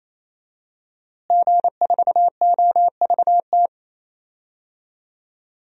Morse code, heard as a keyed tone, about 28 wpm, 700 Hz. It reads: G4OVT
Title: DXCC-100 Call Signs - 5 Characters - Repeated 3x - 28wpm